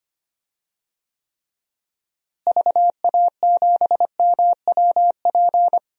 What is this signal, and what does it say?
Morse code, heard as a keyed tone, about 25 wpm, 700 Hz. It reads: VA7MWP
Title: DXCC-100 Call Signs - 6 Characters - Repeated 2x - 25wpm